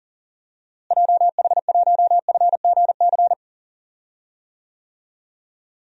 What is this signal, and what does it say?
Morse code, heard as a keyed tone, about 40 wpm, 700 Hz. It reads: JH1FGC
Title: DXCC-100 Call Signs - 6 Characters - Repeated 3x - 40wpm